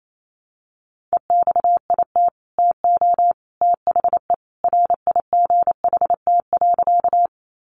Morse code, heard as a keyed tone, about 28 wpm, 700 Hz. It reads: EXIT TO THE RIGHT.
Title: Sentences from Top 1100 Words 28wpm